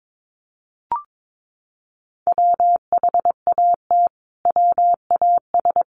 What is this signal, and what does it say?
Morse code, heard as a keyed tone, about 22 wpm, 700 Hz. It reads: WHAT WAS
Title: Sentences from Top 1300 Words 22wpm